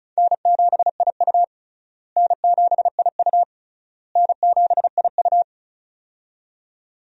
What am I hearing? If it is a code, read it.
Morse code, heard as a keyed tone, about 36 wpm, 700 Hz. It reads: N7IU N7IU N7IU
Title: USA Call Signs - 4 Characters - Repeated 3x - 35wpm